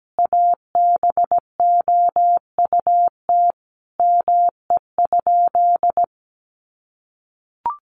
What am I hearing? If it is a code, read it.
Morse code, heard as a keyed tone, about 17 wpm, 700 Hz. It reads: ABOUT ME?
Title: Sentences from Top 1500 Words 17wpm